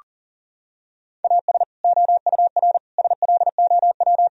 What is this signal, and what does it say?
Morse code, heard as a keyed tone, about 40 wpm, 700 Hz. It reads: AS OUR SLOW
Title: Sentences from Top 1000 Words - Review Part 2 - 40wpm